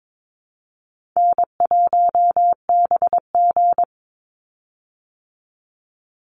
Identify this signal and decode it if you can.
Morse code, heard as a keyed tone, about 22 wpm, 700 Hz. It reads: N1BG
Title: Easy Call Signs Repeated 3x - 22wpm